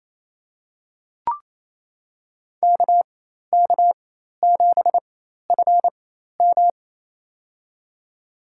Morse code, heard as a keyed tone, about 28 wpm, spaced as for 13 wpm, 700 Hz. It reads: KK7FM
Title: USA Call Signs - 4x Character Spacing - 28wpm